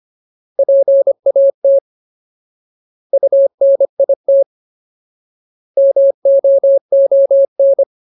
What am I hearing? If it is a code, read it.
Morse code, heard as a keyed tone, about 25 wpm, 550 Hz. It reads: PAT UNIT MOON
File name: Ultimate Word List / 25WPM Ultimate